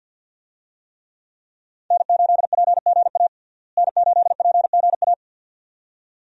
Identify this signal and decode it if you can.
Morse code, heard as a keyed tone, about 48 wpm, 700 Hz. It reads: N8PGA N8PGA
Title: USA Call Signs - 5 Characters - Repeated 2x - 50wpm